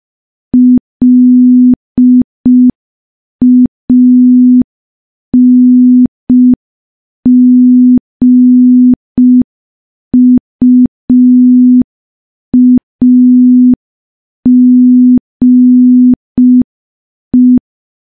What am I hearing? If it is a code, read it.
Morse code, heard as a keyed tone, about 5 wpm, 250 Hz. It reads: LANGUAGE
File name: morse / morse-training-05WPM